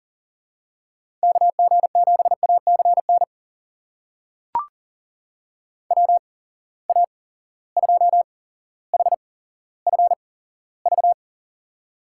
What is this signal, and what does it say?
Morse code, heard as a keyed tone, about 40 wpm, spaced as for 17 wpm, 700 Hz. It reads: KG7ACN WA2HFV
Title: USA Call Signs - 6 Characters - 8x Character Spacing - 40wpm